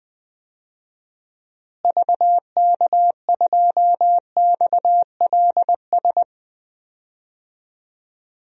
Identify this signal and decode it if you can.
Morse code, heard as a keyed tone, about 20 wpm, 700 Hz. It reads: VK2XLS
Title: DXCC-100 Call Signs - 6 Characters - Repeated 3x - 20wpm